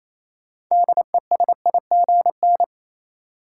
Morse code, heard as a keyed tone, about 28 wpm, 700 Hz. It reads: DESIGN
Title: Sentences from Top 800 Words - New Word Drill - 28wpm